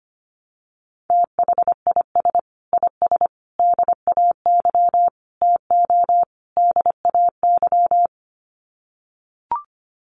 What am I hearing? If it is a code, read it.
Morse code, heard as a keyed tone, about 25 wpm, 700 Hz. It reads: THIS IS DAY TO DAY
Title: Sets of 5 Words Encore 25wpm